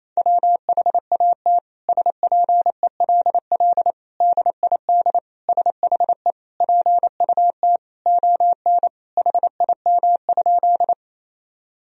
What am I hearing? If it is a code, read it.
Morse code, heard as a keyed tone, about 28 wpm, 700 Hz. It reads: WHAT SPELL DID SHE PUT ON HIM?
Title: Sentences from Top 1000 Words - Review Part 1 - 28wpm